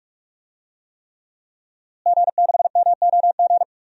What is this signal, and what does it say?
Morse code, heard as a keyed tone, about 46 wpm, 700 Hz. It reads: G6MOG
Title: DXCC-100 Call Signs - 5 Characters - Repeated 3x - 45wpm